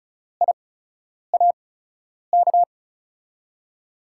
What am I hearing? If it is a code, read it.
Morse code, heard as a keyed tone, about 36 wpm, spaced as for 10 wpm, 700 Hz. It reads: IAK
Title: USA Call Signs - 8x Character Spacing - 35wpm